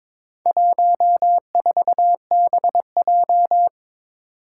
Morse code, heard as a keyed tone, about 23 wpm, 700 Hz. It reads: 14BJ
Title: Four Challenging Letter-Number Combinations 22wpm